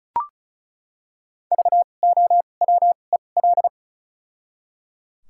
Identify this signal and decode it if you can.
Morse code, heard as a keyed tone, about 34 wpm, spaced as for 25 wpm, 700 Hz. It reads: VOWEL
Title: 2x Character Spacing - Top 500 Words - 35wpm